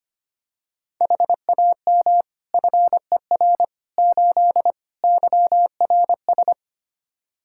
Morse code, heard as a keyed tone, about 25 wpm, 700 Hz. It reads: HAM FER 8 YRS
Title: Practice QSO - Exchange 3 - More Info - 25wpm